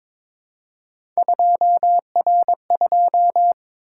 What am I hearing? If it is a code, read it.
Morse code, heard as a keyed tone, about 22 wpm, 700 Hz. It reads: 2R2
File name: practice / 22wpm / Three Letter-Number Combinations - 22wpm